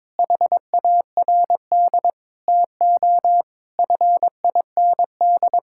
Morse code, heard as a keyed tone, about 21 wpm, 700 Hz. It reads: HARD TO FIND
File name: practice / 22wpm / Sentences with 3-Word Phrases - 22wpm